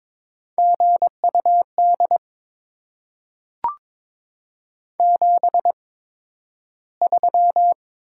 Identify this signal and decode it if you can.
Morse code, heard as a keyed tone, about 22 wpm, spaced as for 12 wpm, 700 Hz. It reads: GUD 73
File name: Top 20 QSO Elements - 8x Character Spacing - 22wpm